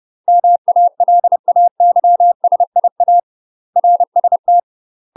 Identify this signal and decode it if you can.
Morse code, heard as a keyed tone, about 30 wpm, 700 Hz. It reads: MALAYSIA RST
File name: UK MASTER QSOs TRAINING / 30WPM / UK MASTER QSOs 0023